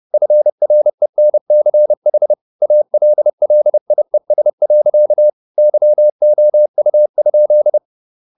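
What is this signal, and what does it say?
Morse code, heard as a keyed tone, about 30 wpm, 600 Hz. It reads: FRENCH ALLIES. YOU?
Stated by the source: Morse code at 30 wpm, the novel Never by Ken Follett in English, from CW med Gnister.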